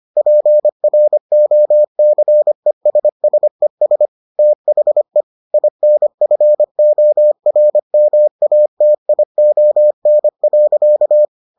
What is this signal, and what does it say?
Morse code, heard as a keyed tone, about 25 wpm, 600 Hz. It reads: PROCESSES THE INFORMATION.